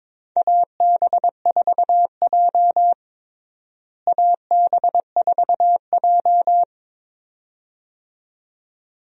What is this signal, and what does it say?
Morse code, heard as a keyed tone, about 22 wpm, 700 Hz. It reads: AB4J AB4J
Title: USA Call Signs - 4 Characters - Repeated 2x - 22wpm